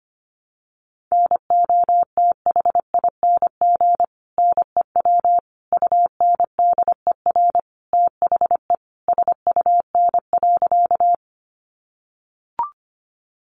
Morse code, heard as a keyed tone, about 25 wpm, 700 Hz. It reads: NOTHING NEW UNDER THE SUN.
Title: Sentences from Top 500 Words 25wpm